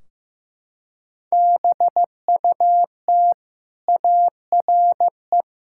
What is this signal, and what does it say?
Morse code, heard as a keyed tone, about 15 wpm, 700 Hz. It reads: BUT ARE
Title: Sentences from Top 1600 Words 15wpm